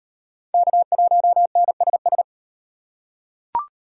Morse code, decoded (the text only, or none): K1NSS